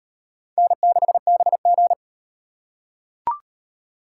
Morse code, heard as a keyed tone, about 34 wpm, 700 Hz. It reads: N6BG